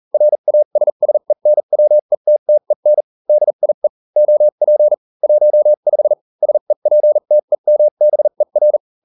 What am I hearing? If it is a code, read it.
Morse code, heard as a keyed tone, about 40 wpm, spaced as for 35 wpm, 600 Hz. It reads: RASSENWETTEN DIE OP 15 SEPTEMBER